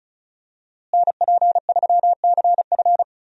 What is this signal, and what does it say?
Morse code, heard as a keyed tone, about 34 wpm, 700 Hz. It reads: NP3CF
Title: USA Call Signs - 5 Characters - Repeated 2x - 35wpm